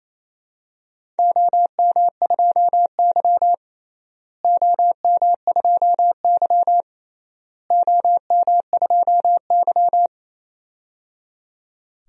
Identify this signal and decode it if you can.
Morse code, heard as a keyed tone, about 27 wpm, 700 Hz. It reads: OM2Y OM2Y OM2Y
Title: DXCC-100 Call Signs - 4 Characters - Repeated 3x - 28wpm